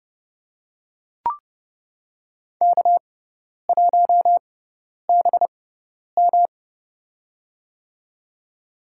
Morse code, heard as a keyed tone, about 30 wpm, spaced as for 10 wpm, 700 Hz. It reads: K1BM
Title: USA Call Signs - 4 Characters - 6x Character Spacing - 30wpm